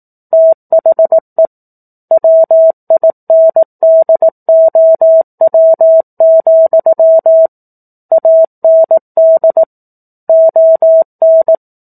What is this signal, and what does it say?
Morse code, heard as a keyed tone, about 18 wpm, 650 Hz. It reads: THE WINDOW, AND ON
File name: 18 WPM